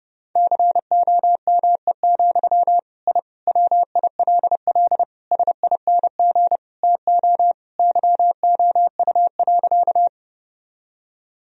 Morse code, heard as a keyed tone, about 30 wpm, 700 Hz. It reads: COME, I WILL SING TO YOU.
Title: Sentences from Top 1000 Words - Review Part 2 - 30wpm